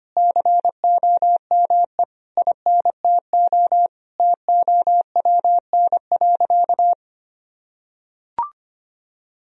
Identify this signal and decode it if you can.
Morse code, heard as a keyed tone, about 25 wpm, 700 Hz. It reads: COME INTO TOWN.
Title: Sentences from Top 1000 Words - Review Part 2 - 25wpm